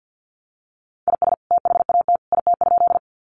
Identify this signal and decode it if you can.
Morse code, heard as a keyed tone, about 50 wpm, 700 Hz. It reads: IS THAT IT?
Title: Sentences from Top 100 Words 50wpm